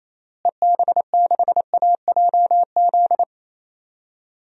Morse code, heard as a keyed tone, about 28 wpm, 700 Hz. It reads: EB6AJZ
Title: DXCC-100 Call Signs - 6 Characters - Repeated 3x - 28wpm